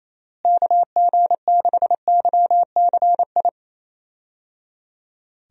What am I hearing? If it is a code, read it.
Morse code, heard as a keyed tone, about 28 wpm, 700 Hz. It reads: KG6YCI
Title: USA Call Signs - 6 Characters - Repeated 3x - 28wpm